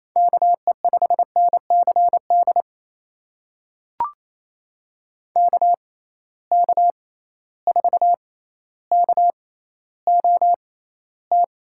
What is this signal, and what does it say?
Morse code, heard as a keyed tone, about 28 wpm, spaced as for 14 wpm, 700 Hz. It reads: KE5NCD KK4KOT
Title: USA Call Signs - 6 Characters - 6x Character Spacing - 28wpm